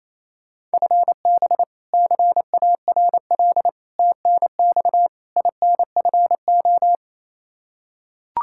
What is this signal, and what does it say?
Morse code, heard as a keyed tone, about 28 wpm, 700 Hz. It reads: FB CARL TNX INFO
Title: Practice QSO - Exchange 3 - More Info - 28wpm